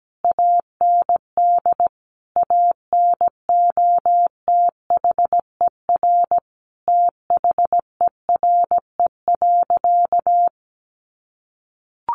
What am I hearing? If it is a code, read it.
Morse code, heard as a keyed tone, about 17 wpm, 700 Hz. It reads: AND ANOTHER THERE.
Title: Sentences from Top 1400 Words 17wpm